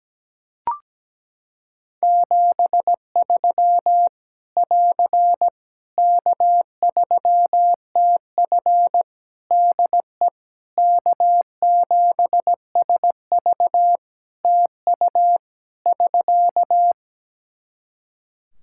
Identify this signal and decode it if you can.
Morse code, heard as a keyed tone, about 17 wpm, 700 Hz. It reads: 73 <AR> K3TF DE K7SV TU <SK>
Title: Practice QSO - Exchange 4 - Ending - 17wpm